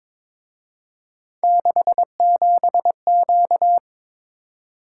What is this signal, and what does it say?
Morse code, heard as a keyed tone, about 27 wpm, 700 Hz. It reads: 67Q
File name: Speed-Racing - Three Letter-Number Combinations - 22wpm